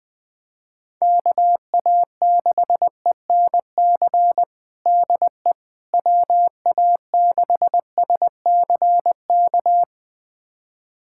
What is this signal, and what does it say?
Morse code, heard as a keyed tone, about 20 wpm, 700 Hz. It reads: KA6ENC DE WA6SCK
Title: USA Callsign DE Callsign - 6 Characters - 20wpm